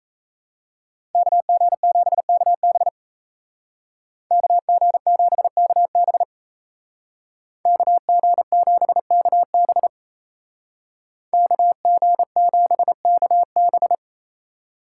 Morse code, heard as a keyed tone, about 35 wpm, 700 Hz. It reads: KG7KB KG7KB KG7KB KG7KB
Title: Speed Racing - USA 2x2 Call Signs 28wpm